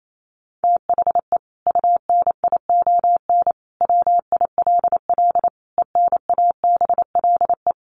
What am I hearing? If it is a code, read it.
Morse code, heard as a keyed tone, about 28 wpm, 700 Hz. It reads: THE UNION WILL ENABLE
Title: Sentences from Top 1500 Words 28wpm